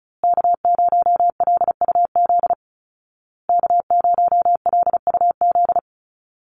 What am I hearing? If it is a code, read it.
Morse code, heard as a keyed tone, about 34 wpm, 700 Hz. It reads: K0LUZ K0LUZ
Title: Call Signs Repeated 2x - 35wpm